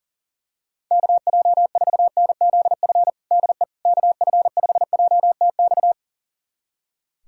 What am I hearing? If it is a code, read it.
Morse code, heard as a keyed tone, about 40 wpm, 700 Hz. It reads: KJ4NZF DE KF5JTX
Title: USA Callsign DE Callsign - 6 Characters - 40wpm